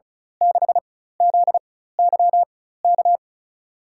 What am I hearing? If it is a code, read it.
Morse code, heard as a keyed tone, about 36 wpm, spaced as for 16 wpm, 700 Hz. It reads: 6ZYK
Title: USA Call Signs - 4x Character Spacing - 35wpm